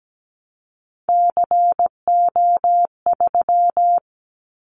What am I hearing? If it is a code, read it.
Morse code, heard as a keyed tone, about 17 wpm, 700 Hz. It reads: CO3
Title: Callsign Prefixes for Top 100 DXCC - 17wpm